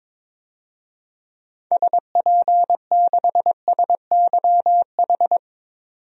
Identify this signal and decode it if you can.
Morse code, heard as a keyed tone, about 22 wpm, 700 Hz. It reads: SP6SYH
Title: DXCC-100 Call Signs - 6 Characters - 22wpm